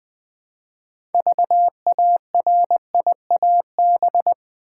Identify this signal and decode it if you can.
Morse code, heard as a keyed tone, about 20 wpm, 700 Hz. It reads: VARIAB
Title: Build A Word - 3 Letters At A Time - 8-Letter Words - 20wpm